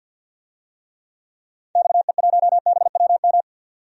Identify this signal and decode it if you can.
Morse code, heard as a keyed tone, about 51 wpm, 700 Hz. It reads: XE1BWM
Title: DXCC-100 Call Signs - 6 Characters - 50wpm